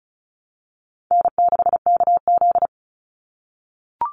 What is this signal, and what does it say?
Morse code, heard as a keyed tone, about 34 wpm, 700 Hz. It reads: N6KZ
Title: NA SOTA Chaser Callsigns 35wpm